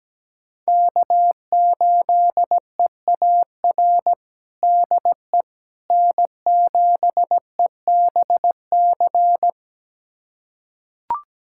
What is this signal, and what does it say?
Morse code, heard as a keyed tone, about 17 wpm, 700 Hz. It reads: K8EAR DE N7EBC
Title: USA Callsign DE Callsign - 5 Characters - 17wpm